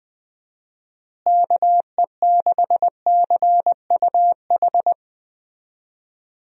Morse code, heard as a keyed tone, about 20 wpm, 700 Hz. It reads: KE6CUH
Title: USA Call Signs - 6 Characters - Repeated 3x - 20wpm